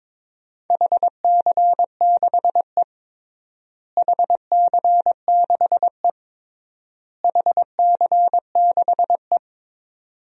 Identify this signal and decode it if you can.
Morse code, heard as a keyed tone, about 22 wpm, 700 Hz. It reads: HC6E HC6E HC6E
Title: DXCC-100 Call Signs - 4 Characters - Repeated 3x - 22wpm